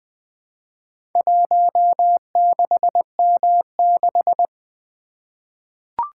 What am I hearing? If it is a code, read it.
Morse code, heard as a keyed tone, about 20 wpm, 700 Hz. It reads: 16M6
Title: Four Letter-Number Combinations - 20wpm